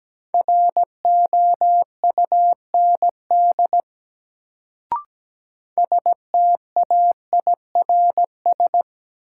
Morse code, heard as a keyed tone, about 17 wpm, 700 Hz. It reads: ROUND STAIRS
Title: Sets of 5 Words Encore - New Word Drill - 17wpm